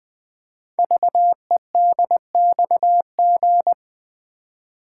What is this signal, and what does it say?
Morse code, heard as a keyed tone, about 20 wpm, 700 Hz. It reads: VEDXG
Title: Code Groups - 5 Letters - 20wpm